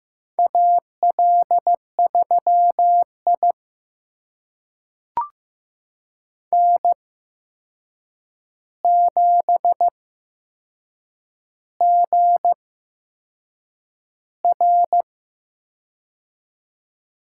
Morse code, heard as a keyed tone, about 15 wpm, spaced as for 6 wpm, 700 Hz. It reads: AL3I N7GR
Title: USA Call Signs - 4 Characters - 8x Character Spacing - 15wpm